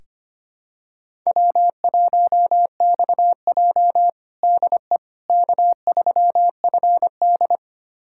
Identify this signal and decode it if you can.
Morse code, heard as a keyed tone, about 25 wpm, 700 Hz. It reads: W1XJ DE K3FD
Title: USA Callsign DE Callsign - 4 Characters - 25wpm